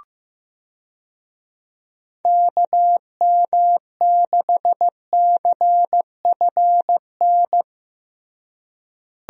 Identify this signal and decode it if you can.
Morse code, heard as a keyed tone, about 15 wpm, 700 Hz. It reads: KM6CFN
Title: USA Call Signs - 6 Characters - Repeated 2x - 15wpm